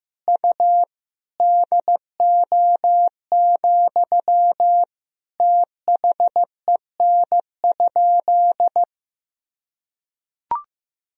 Morse code, heard as a keyed tone, about 16 wpm, 700 Hz. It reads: U DO, THEN?
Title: Sentences from Top 100 Words and Abbreviations 15wpm